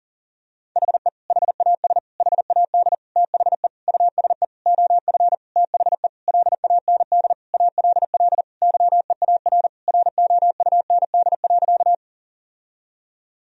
Morse code, heard as a keyed tone, about 40 wpm, 700 Hz. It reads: HE HAS HAD THE USE OF THE LAND ALL YEAR ROUND.
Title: Sentences from Top 1000 Words - Review Part 1 - 40wpm